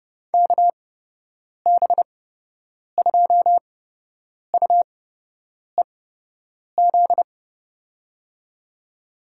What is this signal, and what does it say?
Morse code, heard as a keyed tone, about 30 wpm, spaced as for 8 wpm, 700 Hz. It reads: KB2UEZ
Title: USA Call Signs - 8x Character Spacing - 30wpm